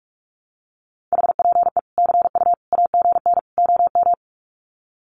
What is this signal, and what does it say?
Morse code, heard as a keyed tone, about 44 wpm, 700 Hz. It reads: HPE CU AGN OM
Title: Common QSO Phrases 45wpm